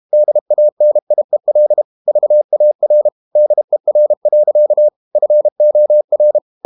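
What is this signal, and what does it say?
Morse code, heard as a keyed tone, about 32 wpm, 600 Hz. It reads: DANIEL VAR DER. FOR
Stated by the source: High-speed QRQ CW at 32 WPM.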